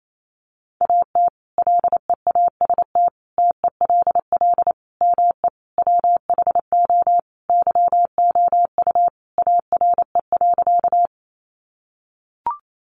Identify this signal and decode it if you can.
Morse code, heard as a keyed tone, about 28 wpm, 700 Hz. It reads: AT LEAST TELL ME WHO YOU ARE.
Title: Sentences from Top 700 Words 28wpm